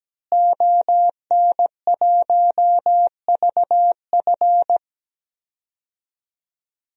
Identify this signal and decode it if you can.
Morse code, heard as a keyed tone, about 17 wpm, 700 Hz. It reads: ON1VF
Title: DXCC-100 Call Signs - 5 Characters - Repeated 3x - 17wpm